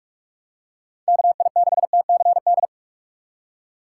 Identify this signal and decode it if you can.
Morse code, heard as a keyed tone, about 46 wpm, 700 Hz. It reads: KI6TCD